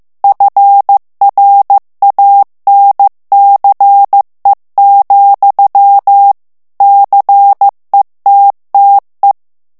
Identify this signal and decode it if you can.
Morse code, heard as a keyed tone, about 15 wpm, 800 Hz. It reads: FRANCE, CETTE
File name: morse mp3 / FRANCAIS TEXTE CLAIR 1 VITESSE 14 MOTS MINUTES